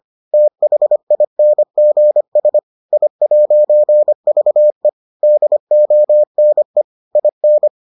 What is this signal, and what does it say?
Morse code, heard as a keyed tone, about 25 wpm, 600 Hz. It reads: THINGS I'VE DONE IN